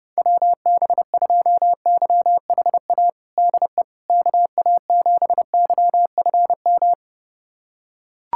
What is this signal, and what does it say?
Morse code, heard as a keyed tone, about 30 wpm, 700 Hz. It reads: WB2YHA DE KA7YFM
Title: USA Callsign DE Callsign - 6 Characters - 30wpm